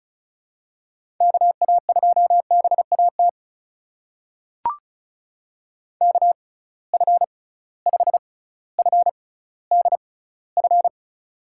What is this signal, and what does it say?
Morse code, heard as a keyed tone, about 36 wpm, spaced as for 18 wpm, 700 Hz. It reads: KA2BAT KF5FDF
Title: USA Call Signs - 6 Characters - 6x Character Spacing - 35wpm